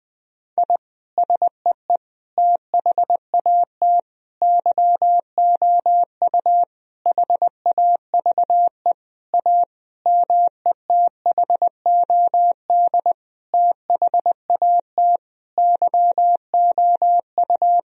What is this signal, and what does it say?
Morse code, heard as a keyed tone, about 20 wpm, 700 Hz. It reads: I SEE THAT YOU HAVE A METHOD THAT YOU
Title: Sentences from Top 1000 Words - Review Part 3 - 20wpm